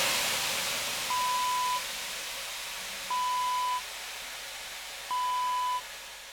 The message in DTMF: none